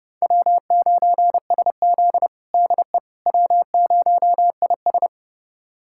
Morse code, heard as a keyed tone, about 30 wpm, 700 Hz. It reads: W9SZ DE W0IS
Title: USA Callsign DE Callsign - 4 Characters - 30wpm